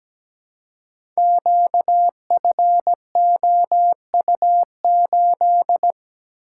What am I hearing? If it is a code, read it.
Morse code, heard as a keyed tone, about 17 wpm, 700 Hz. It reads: QFOU8
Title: Code Groups - 5 Letter:Numbers - 17wpm